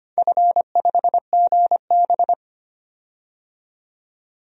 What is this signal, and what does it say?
Morse code, heard as a keyed tone, about 25 wpm, 700 Hz. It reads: F5GB
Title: DXCC-100 Call Signs - 4 Characters - 25wpm